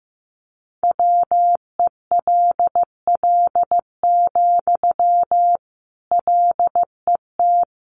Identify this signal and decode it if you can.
Morse code, heard as a keyed tone, about 14 wpm, 700 Hz. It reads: WELL, LET
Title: Sentences from Top 300 Words 15wpm